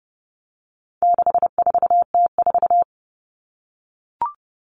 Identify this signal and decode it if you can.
Morse code, heard as a keyed tone, about 30 wpm, 700 Hz. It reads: 64T4
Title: All Cut Numbers - 4-Digit 30wpm